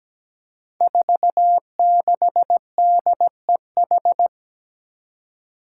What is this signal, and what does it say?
Morse code, heard as a keyed tone, about 17 wpm, 700 Hz. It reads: 46DEH
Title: Five Challenging Letter-Number Combinations 17wpm